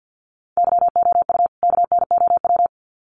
Morse code, heard as a keyed tone, about 50 wpm, 700 Hz. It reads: YOU KNOW
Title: Sets of 2 Words 50wpm